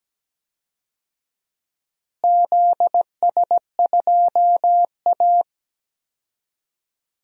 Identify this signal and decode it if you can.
Morse code, heard as a keyed tone, about 17 wpm, 700 Hz. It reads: ZS2A
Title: DXCC-100 Call Signs - 4 Characters - Repeated 2x - 17wpm